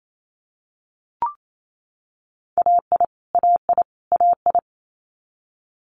Morse code, heard as a keyed tone, about 28 wpm, 700 Hz. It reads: AI AI AI
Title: Multi-letter Phonemes Repeated 3x 28wpm